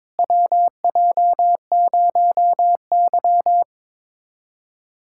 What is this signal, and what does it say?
Morse code, heard as a keyed tone, about 22 wpm, 700 Hz. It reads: WJ0Y
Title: USA Callsign DE Callsign - 4 Characters - 22wpm